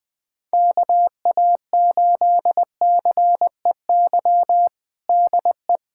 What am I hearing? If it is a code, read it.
Morse code, heard as a keyed tone, about 20 wpm, 700 Hz. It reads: KA8CEY DE
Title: USA Callsign DE Callsign - 6 Characters - 20wpm